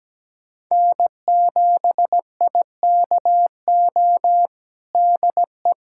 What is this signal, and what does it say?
Morse code, heard as a keyed tone, about 17 wpm, 700 Hz. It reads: N7IKO DE
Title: USA Callsign DE Callsign 17wpm